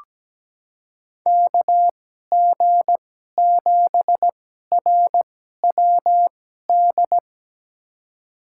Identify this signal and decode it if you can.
Morse code, heard as a keyed tone, about 17 wpm, spaced as for 12 wpm, 700 Hz. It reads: KG7RWD
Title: USA Call Signs - 2x Character Spacing - 17wpm